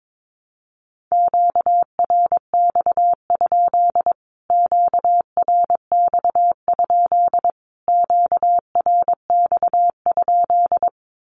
Morse code, heard as a keyed tone, about 22 wpm, 700 Hz. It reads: QRX? QRX? QRX?
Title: Q-Codes Repeated 3x 22wpm